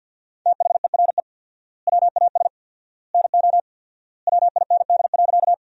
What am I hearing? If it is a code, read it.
Morse code, heard as a keyed tone, about 50 wpm, 700 Hz. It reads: THERE WAS NO WIND.
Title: 4x Word-Spacing - Sentences from Top 400 Words - 50wpm